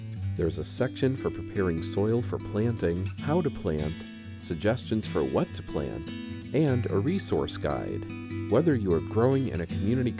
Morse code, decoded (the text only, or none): none